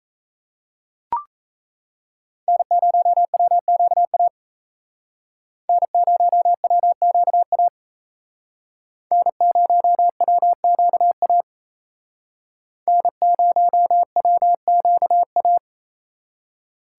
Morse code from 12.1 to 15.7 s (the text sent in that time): N0WQA